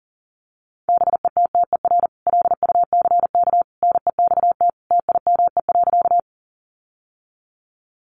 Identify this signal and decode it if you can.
Morse code, heard as a keyed tone, about 40 wpm, 700 Hz. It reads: BETTER LUCK NEXT TIME.
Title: Sentences from Top 1500 Words 40wpm